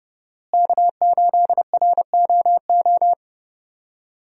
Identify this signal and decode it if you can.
Morse code, heard as a keyed tone, about 30 wpm, 700 Hz. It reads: K8ROO
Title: USA Call Signs - 5 Characters - Repeated 3x - 30wpm